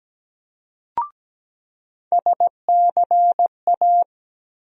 Morse code, heard as a keyed tone, about 17 wpm, 700 Hz. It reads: SCA
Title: USA Call Sign Postfixes - 3 Letters 17wpm